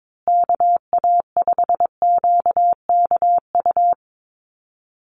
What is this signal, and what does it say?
Morse code, heard as a keyed tone, about 22 wpm, 700 Hz. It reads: KA5QKU
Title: USA Call Signs Repeated 2x 22wpm